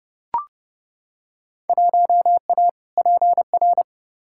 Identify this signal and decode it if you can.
Morse code, heard as a keyed tone, about 30 wpm, 700 Hz. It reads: 1A PR
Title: Field Day Exchanges Encore - 30wpm